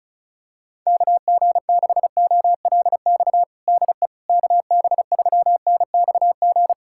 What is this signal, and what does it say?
Morse code, heard as a keyed tone, about 36 wpm, 700 Hz. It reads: KG6OLX DE KB3NXG
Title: USA Callsign DE Callsign - 6 Characters - 35wpm